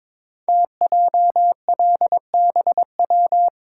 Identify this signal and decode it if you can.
Morse code, heard as a keyed tone, about 22 wpm, 700 Hz. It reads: TJLBW
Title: Code Groups - 5 Letter:Numbers - 22wpm